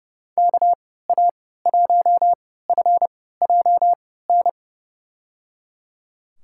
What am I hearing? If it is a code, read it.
Morse code, heard as a keyed tone, about 30 wpm, spaced as for 17 wpm, 700 Hz. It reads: KA1FJN
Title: USA Call Signs - 6 Characters - 3x Character Spacing - 30wpm